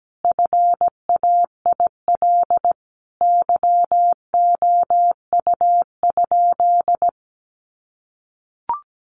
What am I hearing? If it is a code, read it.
Morse code, heard as a keyed tone, about 17 wpm, 700 Hz. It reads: FAIL YOU?
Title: Sentences from Top 1300 Words 17wpm